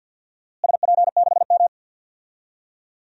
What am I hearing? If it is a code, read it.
Morse code, heard as a keyed tone, about 50 wpm, 700 Hz. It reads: SP6M
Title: DXCC-100 Call Signs - 4 Characters - Repeated 2x - 50wpm